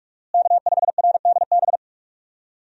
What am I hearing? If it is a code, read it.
Morse code, heard as a keyed tone, about 46 wpm, 700 Hz. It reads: K5RDB
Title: USA Call Signs - 5 Characters - Repeated 3x - 45wpm